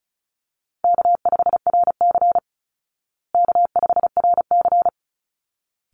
Morse code, heard as a keyed tone, about 34 wpm, 700 Hz. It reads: K5RC K5RC
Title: Call Signs Repeated 3x - 35wpm